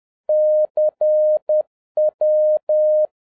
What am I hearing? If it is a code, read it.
Morse code, heard as a keyed tone, about 10 wpm, 610 Hz. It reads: CW